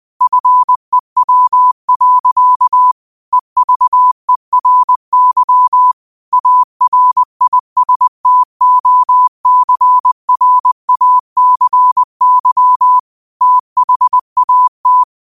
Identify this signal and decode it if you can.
Morse code, heard as a keyed tone, about 20 wpm, 1000 Hz. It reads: FEW. EVERY ARISTOCRACY THAT
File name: qotdfiles / 20wpm